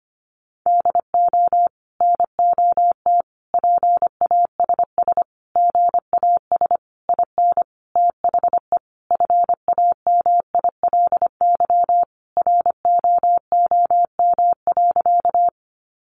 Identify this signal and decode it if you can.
Morse code, heard as a keyed tone, about 25 wpm, 700 Hz. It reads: DO NOT PASS GAS IN THE FAMILY ROOM.
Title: Sentences from Top 1100 Words 25wpm